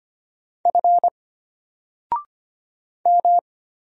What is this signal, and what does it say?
Morse code, heard as a keyed tone, about 25 wpm, 700 Hz. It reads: F M
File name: Single-Letter-Number-25wpm